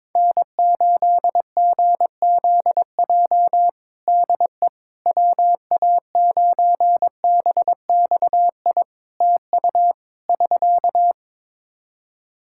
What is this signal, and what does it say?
Morse code, heard as a keyed tone, about 22 wpm, 700 Hz. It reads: N8GZJ DE WA9BXI TU <SK>
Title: K1USN SST QSO Practice 22wpm